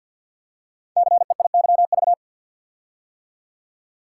Morse code, heard as a keyed tone, about 49 wpm, 700 Hz. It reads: CEIYV
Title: Code Groups - 5 Letter:Numbers - 50wpm